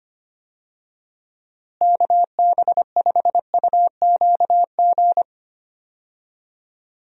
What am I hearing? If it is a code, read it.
Morse code, heard as a keyed tone, about 25 wpm, 700 Hz. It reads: KB5UQG
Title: USA Call Signs - 6 Characters - Repeated 3x - 25wpm